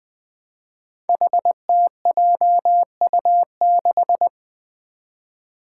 Morse code, heard as a keyed tone, about 20 wpm, 700 Hz. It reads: HTJU6
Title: Five Challenging Letter-Number Combinations 20wpm